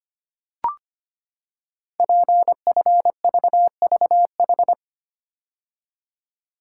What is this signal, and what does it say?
Morse code, heard as a keyed tone, about 25 wpm, 700 Hz. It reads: PFVVH
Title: Code Groups - 5 Letters - 25wpm